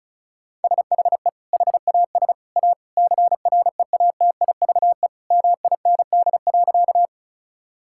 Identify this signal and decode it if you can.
Morse code, heard as a keyed tone, about 34 wpm, 700 Hz. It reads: SHE HAS A CREATIVE MIND.
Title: Sentences from Top 1000 Words - Review Part 5 - 35wpm